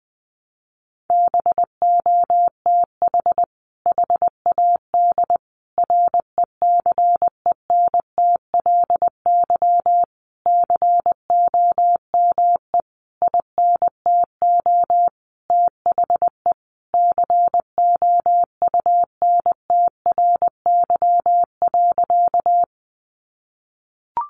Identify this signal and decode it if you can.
Morse code, heard as a keyed tone, about 20 wpm, 700 Hz. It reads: BOTH HAD RECENTLY COME INTO THE COUNTRY.
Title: Sentences from Top 1200 Words 20wpm